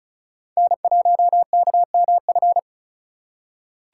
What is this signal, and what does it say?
Morse code, heard as a keyed tone, about 36 wpm, 700 Hz. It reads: N1KMF
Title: USA Call Signs - 5 Characters - 4x Character Spacing - 35wpm